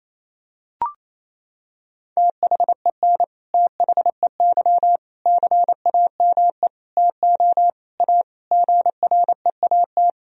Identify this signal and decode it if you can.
Morse code, heard as a keyed tone, about 27 wpm, 700 Hz. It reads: THEN THEY CAME TO A GREAT
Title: Sentences from Top 1000 Words - Review Part 1 - 28wpm